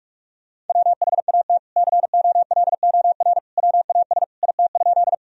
Morse code, heard as a keyed tone, about 46 wpm, 700 Hz. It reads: WHAT COLOR WAS IT?
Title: Sentences from Top 1000 Words - Review Part 2 - 45wpm